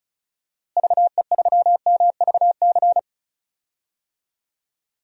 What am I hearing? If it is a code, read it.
Morse code, heard as a keyed tone, about 34 wpm, 700 Hz. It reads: VE3MVC
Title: DXCC-100 Call Signs - 6 Characters - Repeated 3x - 35wpm